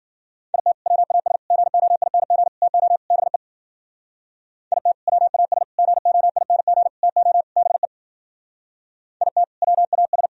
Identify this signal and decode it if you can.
Morse code, heard as a keyed tone, about 53 wpm, 700 Hz. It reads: IT WAS GOING TO BE IT WAS GOING TO BE IT WAS